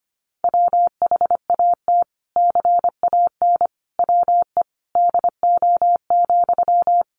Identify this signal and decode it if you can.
Morse code, heard as a keyed tone, about 25 wpm, 700 Hz. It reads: WHAT CAN WE DO,
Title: Sentences from Top 100 Words 25wpm